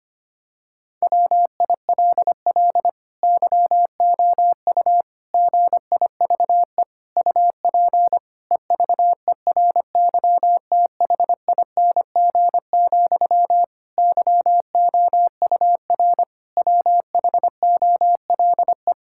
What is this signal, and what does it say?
Morse code, heard as a keyed tone, about 25 wpm, 700 Hz. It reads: WILL YOU GIVE UP EVERYTHING, YOUR WHOLE
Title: Sentences from Top 1000 Words - Review Part 5 - 25wpm